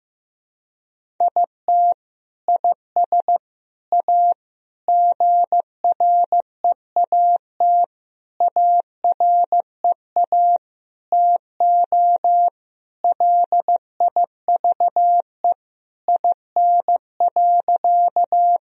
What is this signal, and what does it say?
Morse code, heard as a keyed tone, about 15 wpm, 700 Hz. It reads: IT IS A GREAT AREA TO LIVE IN.
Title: Sentences from Top 1000 Words - Review Part 2 - 15wpm